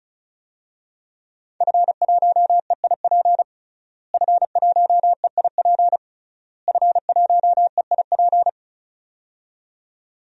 Morse code, heard as a keyed tone, about 34 wpm, 700 Hz. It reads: F1EIP F1EIP F1EIP
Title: DXCC-100 Call Signs - 5 Characters - Repeated 3x - 35wpm